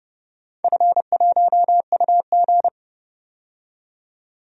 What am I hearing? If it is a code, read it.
Morse code, heard as a keyed tone, about 30 wpm, 700 Hz. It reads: F1UG
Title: DXCC-100 Call Signs - 4 Characters - 30wpm